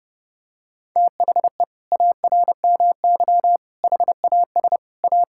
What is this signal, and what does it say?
Morse code, heard as a keyed tone, about 30 wpm, 700 Hz. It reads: THE ARMY HAS A
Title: Sentences from Top 1000 Words - Review Part 5 - 30wpm